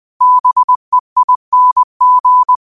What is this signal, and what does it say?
Morse code, heard as a keyed tone, about 20 wpm, 1000 Hz. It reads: BEING
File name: qotdfiles / 20wpm